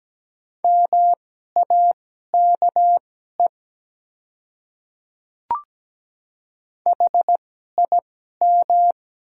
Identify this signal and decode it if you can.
Morse code, heard as a keyed tone, about 17 wpm, spaced as for 12 wpm, 700 Hz. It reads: MAKE HIM
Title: 2x Character Spacing - Top 100 Words - 17wpm